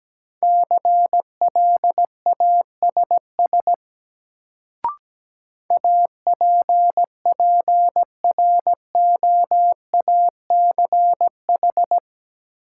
Morse code, heard as a keyed tone, about 17 wpm, 700 Hz. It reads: CLASS APPROACH
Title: Sentences from Top 1400 Words - New Word Drill Alt - 17wpm